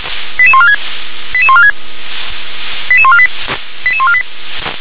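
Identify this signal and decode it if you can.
Five-tone selective call (ZVEI), heard as quick runs of five tones, tones of about 70 ms. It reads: 88146 88146 88148 88148
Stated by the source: ZVEI five-tone selective call received by SDR.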